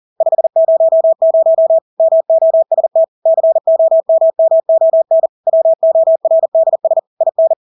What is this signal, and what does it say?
Morse code, heard as a keyed tone, about 40 wpm, 650 Hz. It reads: 500 MOST COMMON WORDS IN
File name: Engelske ord 40-40 wpm